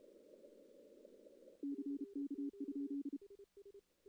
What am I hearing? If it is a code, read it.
Morse code, heard as a keyed tone, about 32 wpm, 280 Hz. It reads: CK?
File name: CW / CW-SS-k5tr- -1700